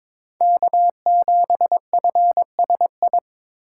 Morse code, heard as a keyed tone, about 22 wpm, 700 Hz. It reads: K7FSI
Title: USA Call Signs - 5 Characters - 2x Character Spacing - 22wpm